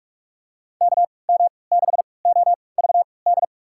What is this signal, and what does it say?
Morse code, heard as a keyed tone, about 47 wpm, spaced as for 26 wpm, 700 Hz. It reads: KM6OVD
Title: USA Call Signs - 6 Characters - 3x Character Spacing - 45wpm